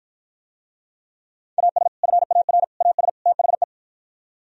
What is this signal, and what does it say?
Morse code, heard as a keyed tone, about 52 wpm, 700 Hz. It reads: AS FAR AS THE